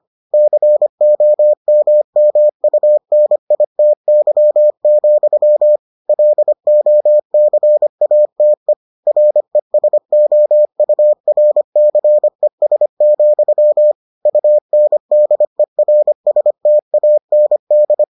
Morse code, heard as a keyed tone, about 25 wpm, 600 Hz. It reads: COMMUNITY, LOCATE RESOURCES, UNDERSTAND